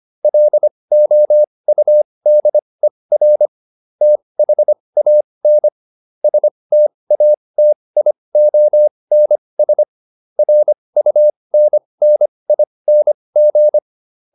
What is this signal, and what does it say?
Morse code, heard as a keyed tone, about 25 wpm, spaced as for 20 wpm, 600 Hz. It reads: LOUDER THAN STATIONS RUNNING